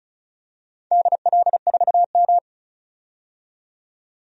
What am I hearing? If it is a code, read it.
Morse code, heard as a keyed tone, about 36 wpm, 700 Hz. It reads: DL4M
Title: DXCC-100 Call Signs - 4 Characters - 35wpm